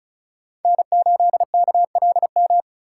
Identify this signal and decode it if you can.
Morse code, heard as a keyed tone, about 34 wpm, 700 Hz. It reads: N8KLM
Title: USA Call Signs - 5 Characters - 4x Character Spacing - 35wpm